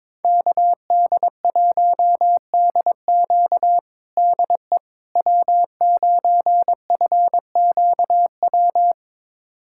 Morse code, heard as a keyed tone, about 22 wpm, 700 Hz. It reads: KD1DQ DE W9FQW
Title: USA Callsign DE Callsign - 5 Characters - 22wpm